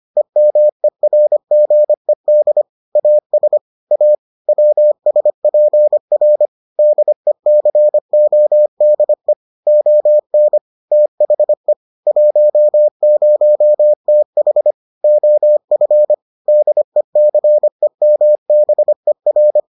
Morse code, heard as a keyed tone, about 25 wpm, 600 Hz. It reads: EMERGED AS A WSPR DECODE ON THE 10TH OF DECEMBER